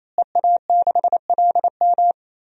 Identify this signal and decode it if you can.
Morse code, heard as a keyed tone, about 28 wpm, 700 Hz. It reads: EA6LM
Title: DXCC-100 Call Signs - 5 Characters - Repeated 3x - 28wpm